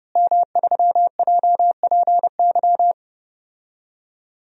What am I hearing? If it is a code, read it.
Morse code, heard as a keyed tone, about 30 wpm, 700 Hz. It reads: M3JPY
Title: DXCC-100 Call Signs - 5 Characters - Repeated 3x - 30wpm